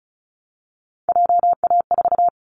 Morse code, heard as a keyed tone, about 34 wpm, 700 Hz. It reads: JA4
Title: Callsign Prefixes for Top 50 DXCC - 35wpm